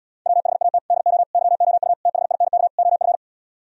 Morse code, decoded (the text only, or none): WHAT DO YOU THINK OF